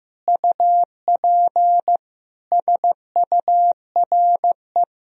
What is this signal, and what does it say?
Morse code, heard as a keyed tone, about 15 wpm, 700 Hz. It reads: UP SURE
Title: Sentences from Top 1000 Words - Review Part 1 - 15wpm